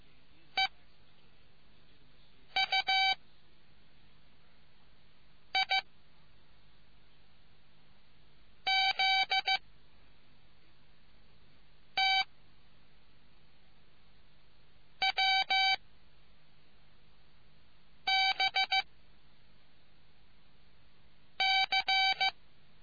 Morse code, heard as a keyed tone, about 15 wpm, 770 Hz. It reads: E U I Z T W B C